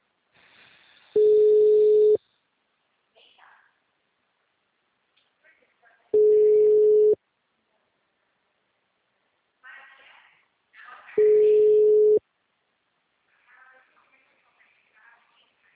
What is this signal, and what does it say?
Ringback tone (Europe, 425 Hz), with speech.